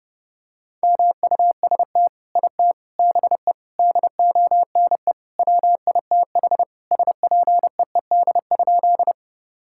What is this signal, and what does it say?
Morse code, heard as a keyed tone, about 30 wpm, 700 Hz. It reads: MUST IT BE DONE WITH SPEED?
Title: Sentences from Top 1000 Words - Review Part 5 - 30wpm